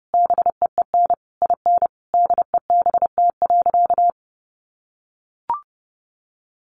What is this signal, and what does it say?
Morse code, heard as a keyed tone, about 30 wpm, 700 Hz. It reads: BEEN IN DEBT.
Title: Sentences from Top 1300 Words 30wpm